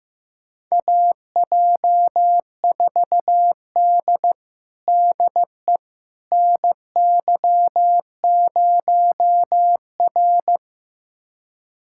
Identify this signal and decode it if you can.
Morse code, heard as a keyed tone, about 15 wpm, 700 Hz. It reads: AJ4D DE NY0R
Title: USA Callsign DE Callsign - 4 Characters - 15wpm